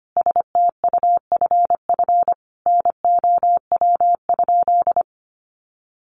Morse code, heard as a keyed tone, about 25 wpm, 700 Hz. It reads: STUFF NOW?
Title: Sentences from Top 1400 Words 25wpm